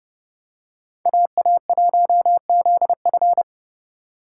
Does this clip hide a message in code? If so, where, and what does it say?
Yes: Morse code from 1.1 to 3.4 s, heard as a keyed tone, about 30 wpm, 700 Hz. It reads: AA1ZF